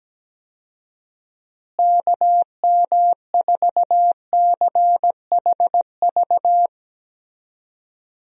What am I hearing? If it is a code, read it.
Morse code, heard as a keyed tone, about 17 wpm, 700 Hz. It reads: KM4CHV
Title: USA Call Signs - 6 Characters - Repeated 2x - 17wpm